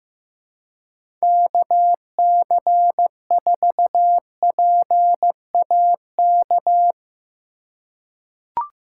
Morse code, heard as a keyed tone, about 15 wpm, 700 Hz. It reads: KC4PAK
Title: USA Call Signs - 6 Characters - 3x Character Spacing - 15wpm